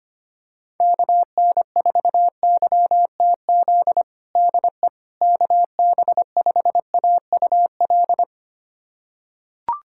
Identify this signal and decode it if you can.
Morse code, heard as a keyed tone, about 25 wpm, 700 Hz. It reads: KN4YTZ DE KB5AUL
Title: USA Callsign DE Callsign - 6 Characters - 25wpm